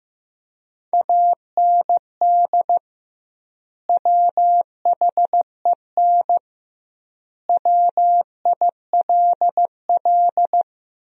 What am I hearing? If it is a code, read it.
Morse code, heard as a keyed tone, about 15 wpm, 700 Hz. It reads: AND WHEN WILL
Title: 2x Word-Spacing - Sentences from Top 100 Words - 15wpm